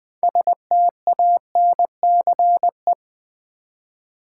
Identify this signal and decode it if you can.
Morse code, heard as a keyed tone, about 20 wpm, 700 Hz. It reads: STANCE
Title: 6 Letter Words - First 3 Letters Given - 20wpm